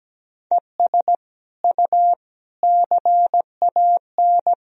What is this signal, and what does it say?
Morse code, heard as a keyed tone, about 17 wpm, 700 Hz. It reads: ES U CAN
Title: Sentences from Top 500 Words and Abbreviations 17wpm